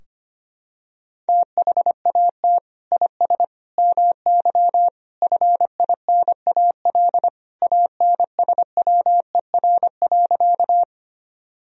Morse code, heard as a keyed tone, about 25 wpm, 700 Hz. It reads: THAT IS MY FINAL ANSWER.
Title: Sentences from Top 1000 Words - Review Part 2 - 25wpm